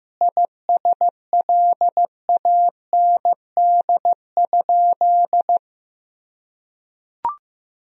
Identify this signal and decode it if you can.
Morse code, heard as a keyed tone, about 15 wpm, 700 Hz. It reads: ISLAND?
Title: Sentences from Top 1000 Words - Review Part 3 - 15wpm